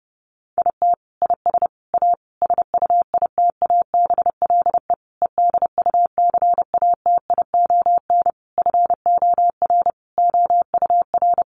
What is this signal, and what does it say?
Morse code, heard as a keyed tone, about 30 wpm, 700 Hz. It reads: IT IS A SUITABLE EDUCATION FOR OUR
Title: Sentences from Top 1400 Words 30wpm